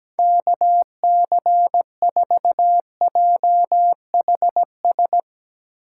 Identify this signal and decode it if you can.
Morse code, heard as a keyed tone, about 17 wpm, 700 Hz. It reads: KC4JHS
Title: USA Call Signs - 6 Characters - 8x Character Spacing - 17wpm